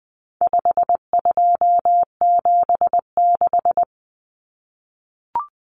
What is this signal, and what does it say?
Morse code, heard as a keyed tone, about 20 wpm, 700 Hz. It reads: 5276
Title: Numbers - 4-Digit 20wpm